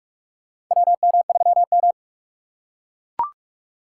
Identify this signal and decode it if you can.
Morse code, heard as a keyed tone, about 41 wpm, 700 Hz. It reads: WM3M
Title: USA Call Signs - 4 Characters - 2x Character Spacing - 45wpm